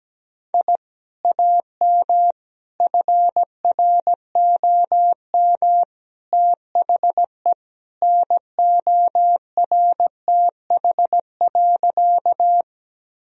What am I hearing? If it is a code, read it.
Morse code, heard as a keyed tone, about 17 wpm, 700 Hz. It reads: I AM FROM THE NORTH.
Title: Sentences from Top 1000 Words - Review Part 1 - 17wpm